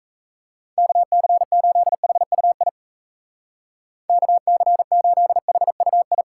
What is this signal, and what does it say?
Morse code, heard as a keyed tone, about 34 wpm, 700 Hz. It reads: KC8HUI KC8HUI
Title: Speed Racing - USA 2x3 Call Signs 28wpm